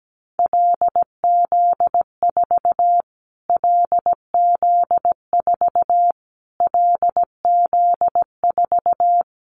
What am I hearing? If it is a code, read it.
Morse code, heard as a keyed tone, about 17 wpm, 700 Hz. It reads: LZ4 LZ4 LZ4
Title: Callsign Prefixes for Top 25 DXCC - Repeated 3x - 17wpm